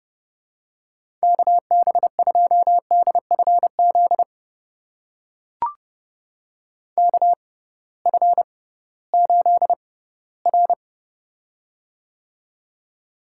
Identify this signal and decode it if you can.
Morse code, heard as a keyed tone, about 30 wpm, spaced as for 17 wpm, 700 Hz. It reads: KB2DFZ KF8R
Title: USA Call Signs - 6x Character Spacing - 30wpm